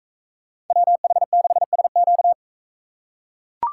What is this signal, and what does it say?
Morse code, heard as a keyed tone, about 43 wpm, 700 Hz. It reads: WH6SQ